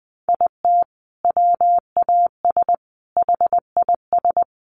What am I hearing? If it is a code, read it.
Morse code, heard as a keyed tone, about 20 wpm, 700 Hz. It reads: IT WAS HIS
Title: Sentences from Top 1200 Words 20wpm